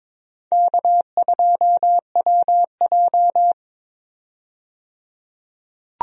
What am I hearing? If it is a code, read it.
Morse code, heard as a keyed tone, about 22 wpm, 700 Hz. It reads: K2WJ
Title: USA Call Signs - 4 Characters - 22wpm